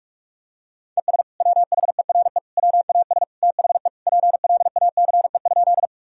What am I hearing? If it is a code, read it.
Morse code, heard as a keyed tone, about 46 wpm, 700 Hz. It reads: ES WHERE WAS THE PLACE?
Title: Sentences from Top 200 Words and Abbreviations 45wpm